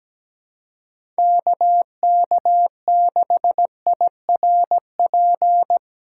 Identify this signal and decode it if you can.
Morse code, heard as a keyed tone, about 17 wpm, 700 Hz. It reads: KK6IRP
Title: USA Call Signs - 6 Characters - 3x Character Spacing - 17wpm